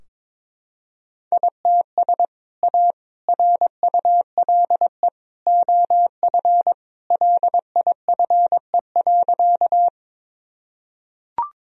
Morse code, heard as a keyed tone, about 22 wpm, 700 Hz. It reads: ITS A RULE OF LIFE.
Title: Sentences from Top 1000 Words - Review Part 2 - 22wpm